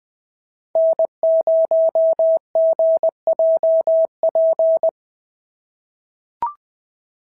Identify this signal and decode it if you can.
Morse code, heard as a keyed tone, about 20 wpm, 650 Hz. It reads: N0GJP
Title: SOTA QSO Practice 20wpm